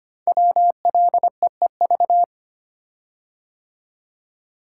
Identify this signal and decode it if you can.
Morse code, heard as a keyed tone, about 25 wpm, 700 Hz. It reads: WLEEV
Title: Code Groups - 5 Letters - 25wpm